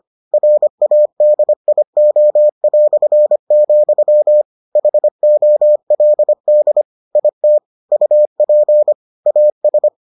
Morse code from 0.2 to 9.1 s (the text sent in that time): RADIO", HOLD IT UP